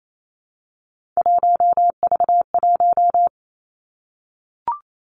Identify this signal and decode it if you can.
Morse code, heard as a keyed tone, about 28 wpm, 700 Hz. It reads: 1V1
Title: Three Challenging Letter-Number Combinations 28wpm